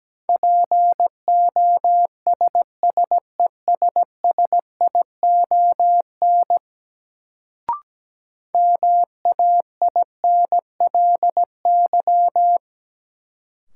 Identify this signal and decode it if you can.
Morse code, heard as a keyed tone, about 17 wpm, 700 Hz. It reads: POSSESSION MAINLY
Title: Sentences from Top 1400 Words - New Word Drill Alt - 17wpm